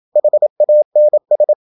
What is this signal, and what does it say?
Morse code, heard as a keyed tone, about 27 wpm, 600 Hz. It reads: HANS
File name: Ferie QRQ 27wpm